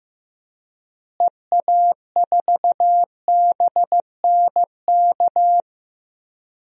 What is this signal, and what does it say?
Morse code, heard as a keyed tone, about 16 wpm, 700 Hz. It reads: EA4BNK
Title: DXCC-100 Call Signs - 6 Characters - Repeated 2x - 15wpm